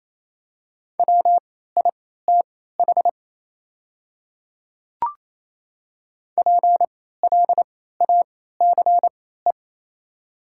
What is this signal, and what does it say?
Morse code, heard as a keyed tone, about 27 wpm, spaced as for 16 wpm, 700 Hz. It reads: WITH PLACE
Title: 3x Character Spacing - Top 200 Words - 28wpm